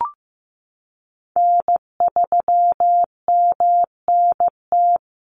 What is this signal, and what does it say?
Morse code, heard as a keyed tone, about 15 wpm, 700 Hz. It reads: N3MNT
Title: Call Signs Repeated 2x - 15wpm